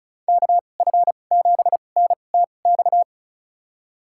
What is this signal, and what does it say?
Morse code, heard as a keyed tone, about 36 wpm, spaced as for 25 wpm, 700 Hz. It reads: KF7NTX
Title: USA Call Signs - 6 Characters - 2x Character Spacing - 35wpm